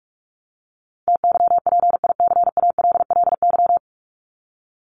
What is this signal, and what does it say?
Morse code, heard as a keyed tone, about 44 wpm, 700 Hz. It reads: TYPICALLY
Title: LY Suffix - Top 100 Words with Base Words 45wpm